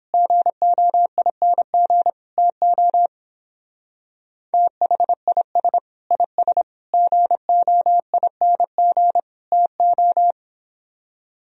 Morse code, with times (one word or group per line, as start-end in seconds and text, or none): 0.1-2.1 GOING
2.4-3.1 TO
4.5-5.8 THIS
6.1-6.6 IS
6.9-9.2 GOING
9.5-10.3 TO